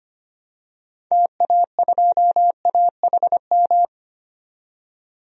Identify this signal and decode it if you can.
Morse code, heard as a keyed tone, about 25 wpm, 700 Hz. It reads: TA2AHM
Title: DXCC-100 Call Signs - 6 Characters - 25wpm